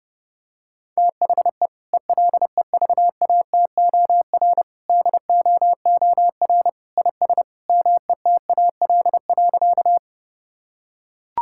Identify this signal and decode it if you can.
Morse code, heard as a keyed tone, about 30 wpm, 700 Hz. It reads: THE ELEVATOR DOOR IS METAL.
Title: Sentences from Top 1600 Words 30wpm